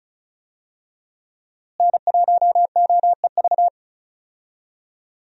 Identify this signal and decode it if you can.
Morse code, heard as a keyed tone, about 35 wpm, 700 Hz. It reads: N1OEV
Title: USA Call Signs - 5 Characters - Repeated 3x - 35wpm